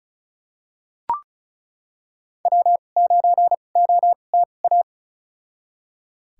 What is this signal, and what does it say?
Morse code, heard as a keyed tone, about 34 wpm, spaced as for 25 wpm, 700 Hz. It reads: W9OTA
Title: USA Call Signs - 2x Character Spacing - 35wpm